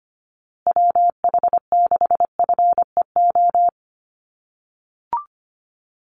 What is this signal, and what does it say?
Morse code, heard as a keyed tone, about 25 wpm, 700 Hz. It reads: WH6FEO
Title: USA Call Signs Repeated 3x 25wpm